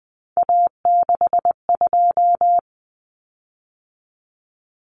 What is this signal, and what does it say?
Morse code, heard as a keyed tone, about 20 wpm, 700 Hz. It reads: A62
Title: Common Cut Numbers - 3-Digit 20wpm